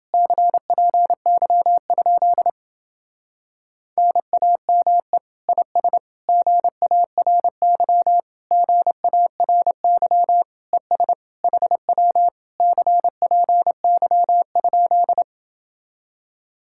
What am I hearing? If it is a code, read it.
Morse code, heard as a keyed tone, about 29 wpm, 700 Hz. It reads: CPY? NAME IS GARY GARY ES HW CPY?